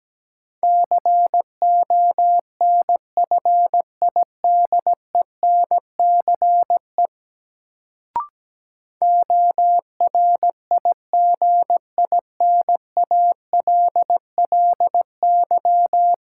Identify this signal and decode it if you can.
Morse code, heard as a keyed tone, about 17 wpm, 700 Hz. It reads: CONFIDENCE ORIGINALLY
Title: Sentences from Top 1500 Words - New Word Drill - 17wpm